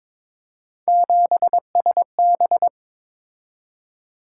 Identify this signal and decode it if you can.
Morse code, heard as a keyed tone, about 22 wpm, 700 Hz. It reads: 7SB
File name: practice / 22wpm / Three Letter-Number Combinations - 22wpm